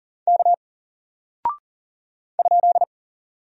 Morse code, heard as a keyed tone, about 38 wpm, 700 Hz. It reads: K ?
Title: Single Character (Letter:Number:Punctuation:Prosign) Alt 40wpm